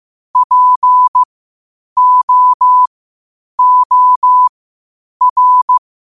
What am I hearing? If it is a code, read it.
Morse code, heard as a keyed tone, about 15 wpm, spaced as for 8 wpm, 1000 Hz. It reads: POOR
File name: qotdfiles / 8wpm